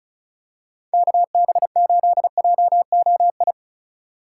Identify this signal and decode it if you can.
Morse code, heard as a keyed tone, about 34 wpm, 700 Hz. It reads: KB8JOI
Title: USA Call Signs - 6 Characters - 3x Character Spacing - 35wpm